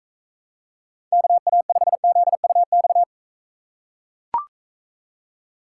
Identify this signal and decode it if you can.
Morse code, heard as a keyed tone, about 44 wpm, 700 Hz. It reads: KA5ZUX